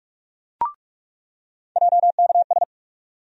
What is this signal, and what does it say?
Morse code, heard as a keyed tone, about 41 wpm, 700 Hz. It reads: JKS
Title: USA Call Sign Postfixes - 3 Letters 45wpm